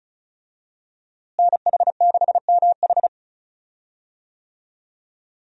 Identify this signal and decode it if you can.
Morse code, heard as a keyed tone, about 34 wpm, 700 Hz. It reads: NH6MH
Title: USA Call Signs - 5 Characters - Repeated 2x - 35wpm